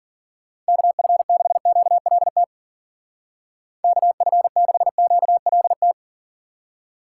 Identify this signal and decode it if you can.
Morse code, heard as a keyed tone, about 49 wpm, 700 Hz. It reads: KF6QLT KF6QLT
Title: Speed Racing - USA Call Signs 40wpm